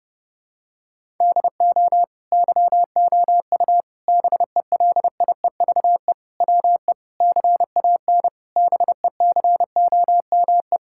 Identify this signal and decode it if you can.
Morse code, heard as a keyed tone, about 32 wpm, 700 Hz. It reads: DO YOU BELIEVE WE CAN BECOME
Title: Sentences from Top 1000 Words - Review Part 6 - 30wpm